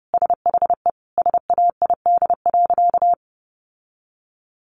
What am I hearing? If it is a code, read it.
Morse code, heard as a keyed tone, about 30 wpm, 700 Hz. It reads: SHE SAID.
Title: Sentences from Top 1000 Words 30wpm